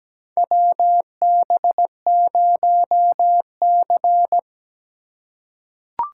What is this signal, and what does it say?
Morse code, heard as a keyed tone, about 17 wpm, 700 Hz. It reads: WB0C
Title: USA Call Signs - 4 Characters - 3x Character Spacing - 17wpm